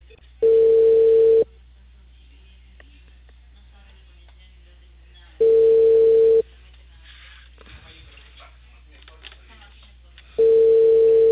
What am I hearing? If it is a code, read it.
Ringback tone (China, 450 Hz), with speech.